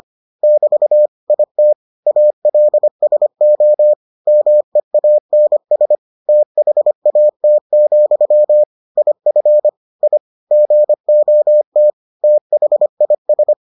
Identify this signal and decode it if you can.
Morse code, heard as a keyed tone, about 25 wpm, 600 Hz. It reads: <BT> IT ALSO MEANS THAT, IF I GOT THIS